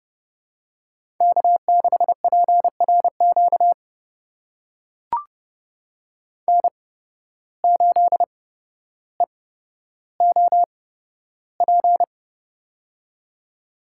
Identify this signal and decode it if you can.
Morse code, heard as a keyed tone, about 30 wpm, spaced as for 13 wpm, 700 Hz. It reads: K6PRQ N8EOP
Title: USA Call Signs - 5 Characters - 8x Character Spacing - 30wpm